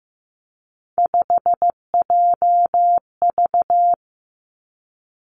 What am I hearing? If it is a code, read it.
Morse code, heard as a keyed tone, about 15 wpm, 700 Hz. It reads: 5JV
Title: Three Challenging Letter-Number Combinations 15wpm